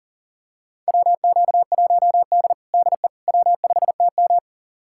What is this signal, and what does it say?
Morse code, heard as a keyed tone, about 37 wpm, 700 Hz. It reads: WQ1D DE W5TM
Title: USA Callsign DE Callsign - 4 Characters - 40wpm